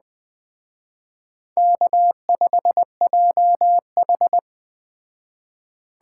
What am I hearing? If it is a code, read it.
Morse code, heard as a keyed tone, about 20 wpm, 700 Hz. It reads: K5JH
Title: Speed Racing - USA 1x2 Call Signs 17wpm